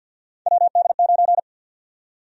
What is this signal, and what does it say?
Morse code, heard as a keyed tone, about 52 wpm, 700 Hz. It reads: WD9
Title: USA Call Sign Prefixes - 50wpm